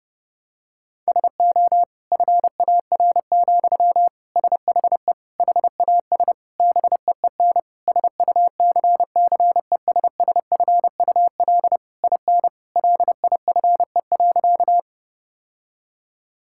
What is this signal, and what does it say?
Morse code, heard as a keyed tone, about 32 wpm, 700 Hz. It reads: SO FAR, SHE HAS BEEN SUCCESSFUL IN LIFE.
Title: Sentences from Top 1000 Words - Review Part 6 - 30wpm